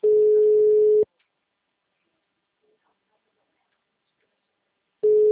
Ringback tone (Europe, 425 Hz).